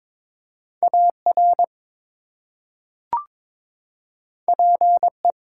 Morse code, heard as a keyed tone, about 22 wpm, 700 Hz. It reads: AR PE
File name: ARRL Sections v2 - 22wpm